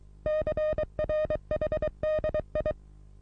Morse code, heard as a keyed tone, about 24 wpm, 620 Hz. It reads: CRHDI
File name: repos / Morsekurs / DJ4UF Morsen27